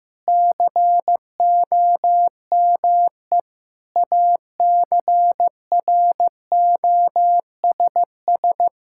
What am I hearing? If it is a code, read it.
Morse code, heard as a keyed tone, about 15 wpm, 700 Hz. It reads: COME ACROSS
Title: Sentences from Top 1000 Words - Review Part 5 - 15wpm